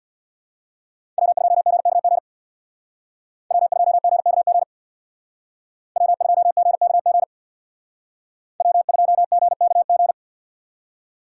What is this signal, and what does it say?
Morse code, heard as a keyed tone, about 60 wpm, 700 Hz. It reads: W2GKG W2GKG W2GKG W2GKG
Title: Speed Racing - USA 1x3 Call Signs 50wpm